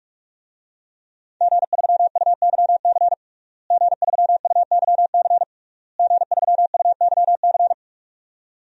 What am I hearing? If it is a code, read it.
Morse code, heard as a keyed tone, about 46 wpm, 700 Hz. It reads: G3UYC G3UYC G3UYC
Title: DXCC-100 Call Signs - 5 Characters - Repeated 3x - 45wpm